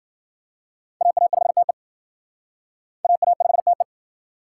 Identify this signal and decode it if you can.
Morse code, heard as a keyed tone, about 60 wpm, 700 Hz. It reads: AA5TE AA5TE